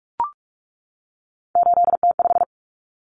Morse code, heard as a keyed tone, about 44 wpm, 700 Hz. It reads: 8T5
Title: Common Cut Numbers - 3-Digit 45wpm